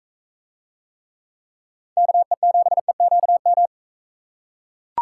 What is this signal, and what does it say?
Morse code, heard as a keyed tone, about 42 wpm, 700 Hz. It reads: KE7EQM